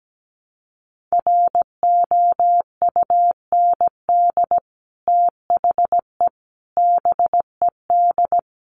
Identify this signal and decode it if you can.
Morse code, heard as a keyed tone, about 17 wpm, 700 Hz. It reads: ROUND THE BED
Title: Sentences from Top 1000 Words 17wpm